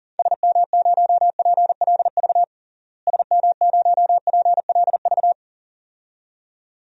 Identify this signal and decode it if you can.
Morse code, heard as a keyed tone, about 40 wpm, 700 Hz. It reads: SM0PLV SM0PLV
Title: DXCC-100 Call Signs - 6 Characters - Repeated 2x - 40wpm